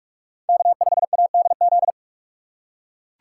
Morse code, heard as a keyed tone, about 44 wpm, 700 Hz. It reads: K5ADZ
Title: USA Call Signs - 6x Character Spacing - 45wpm